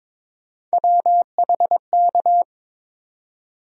Morse code, heard as a keyed tone, about 21 wpm, 700 Hz. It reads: WHK
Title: USA Call Sign Postfixes - 3 Letters 22wpm